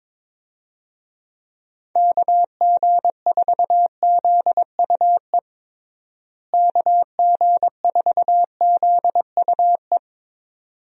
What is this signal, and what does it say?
Morse code, heard as a keyed tone, about 22 wpm, 700 Hz. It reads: KG4ZUE KG4ZUE
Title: USA Call Signs - 6 Characters - Repeated 2x - 22wpm